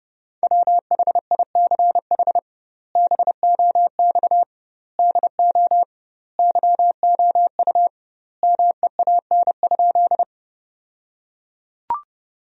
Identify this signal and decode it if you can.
Morse code, heard as a keyed tone, about 30 wpm, 700 Hz. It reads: WHICH BOX DO YOU MEAN?
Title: 2x Word-Spacing - Sentences from Top 500 Words - 30wpm